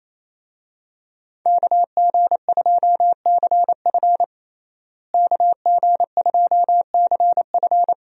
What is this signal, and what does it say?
Morse code, heard as a keyed tone, about 28 wpm, 700 Hz. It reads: KG2CF KG2CF
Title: USA Call Signs - 5 Characters - Repeated 3x - 28wpm